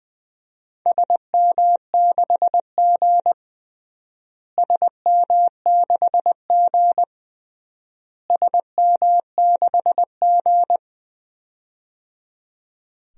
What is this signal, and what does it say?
Morse code, heard as a keyed tone, about 20 wpm, 700 Hz. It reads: SM6G SM6G SM6G
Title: DXCC-100 Call Signs - 4 Characters - Repeated 3x - 20wpm